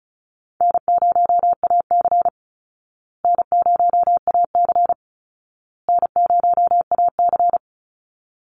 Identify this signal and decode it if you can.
Morse code, heard as a keyed tone, about 34 wpm, 700 Hz. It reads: N0AC N0AC N0AC
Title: Easy Call Signs Repeated 3x - 35wpm